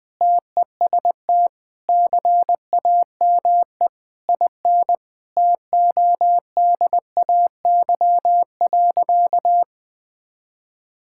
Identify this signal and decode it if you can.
Morse code, heard as a keyed tone, about 20 wpm, 700 Hz. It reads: TEST CAME IN TODAY.
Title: Sentences with 4-Word Phrases - 20wpm